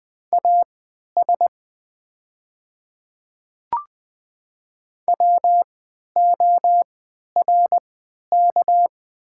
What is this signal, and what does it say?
Morse code, heard as a keyed tone, about 20 wpm, spaced as for 11 wpm, 700 Hz. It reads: AS WORK
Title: 3x Character Spacing - Top 200 Words - 20wpm